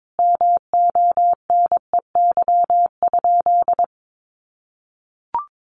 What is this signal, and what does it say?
Morse code, heard as a keyed tone, about 22 wpm, 700 Hz. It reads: MONEY?
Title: Sentences from Top 400 Words 22wpm